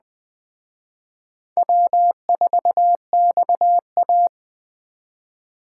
Morse code, heard as a keyed tone, about 20 wpm, 700 Hz. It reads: W4XA